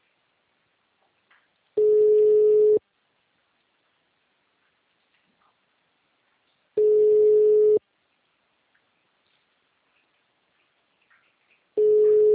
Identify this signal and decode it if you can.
Ringback tone (Europe, 425 Hz).
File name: out- -2009- - - .2455